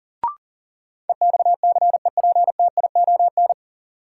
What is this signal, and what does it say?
Morse code, heard as a keyed tone, about 38 wpm, 700 Hz. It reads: EXCEPTION
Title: Sets of 4 Words - New Word Drill - 40wpm